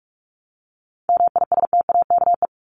Morse code, heard as a keyed tone, about 44 wpm, 700 Hz. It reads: MISTAKE
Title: Top 1501-1750 Words 45wpm